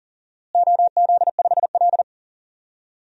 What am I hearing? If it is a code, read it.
Morse code, heard as a keyed tone, about 40 wpm, 700 Hz. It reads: OZ5L
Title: DXCC-100 Call Signs - 4 Characters - Repeated 3x - 40wpm